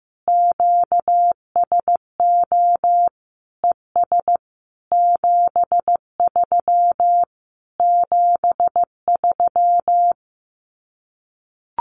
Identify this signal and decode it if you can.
Morse code, heard as a keyed tone, about 15 wpm, 700 Hz. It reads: QSO ES 73 73
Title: Common QSO Phrases Encore 15wpm